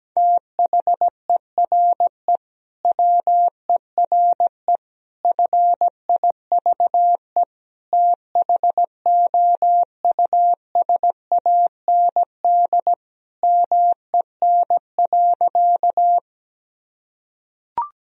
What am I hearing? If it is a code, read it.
Morse code, heard as a keyed tone, about 17 wpm, 700 Hz. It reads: THERE WERE FIVE THOUSAND MEN.
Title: Sentences from Top 1000 Words - Review Part 3 - 17wpm